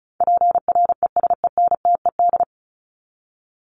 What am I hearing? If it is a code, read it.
Morse code, heard as a keyed tone, about 34 wpm, 700 Hz. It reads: PRESENTED
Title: ED Suffix - Top 100 Words with Base Words 35wpm